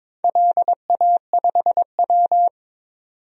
Morse code, heard as a keyed tone, about 22 wpm, 700 Hz. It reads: LA5W
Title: DXCC-100 Call Signs - 4 Characters - 22wpm